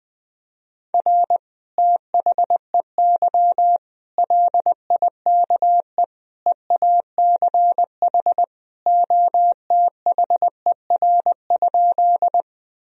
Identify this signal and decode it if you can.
Morse code, heard as a keyed tone, about 21 wpm, 700 Hz. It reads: R THEY LIKE EACH OTHER?
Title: Sentences from Top 200 Words and Abbreviations 20wpm